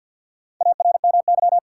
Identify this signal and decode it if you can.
Morse code, heard as a keyed tone, about 47 wpm, 700 Hz. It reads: ARMY
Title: Four Letter Words - 50wpm